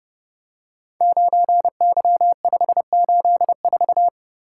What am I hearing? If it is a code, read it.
Morse code, heard as a keyed tone, about 30 wpm, 700 Hz. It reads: 9Y584
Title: Code Groups - 5 Letter:Numbers - 30wpm